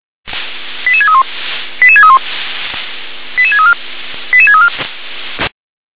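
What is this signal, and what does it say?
Five-tone selective call (ZVEI), heard as quick runs of five tones, tones of about 70 ms. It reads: 88631 88631 88634 88634
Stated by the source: ZVEI five-tone selective call received on an SDR.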